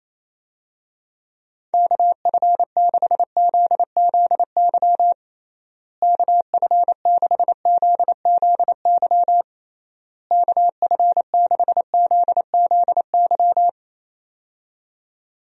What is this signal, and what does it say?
Morse code, heard as a keyed tone, about 28 wpm, 700 Hz. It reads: KF6ZZY KF6ZZY KF6ZZY
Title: USA Call Signs - 6 Characters - Repeated 3x - 28wpm